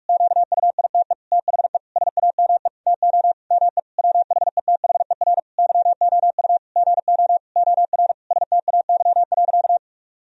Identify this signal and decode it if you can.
Morse code, heard as a keyed tone, about 41 wpm, 700 Hz. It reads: QUITE THE SAME TO ME WHETHER YOU GO OR STAY.
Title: Sentences from Top 1000 Words - Review Part 3 - 45wpm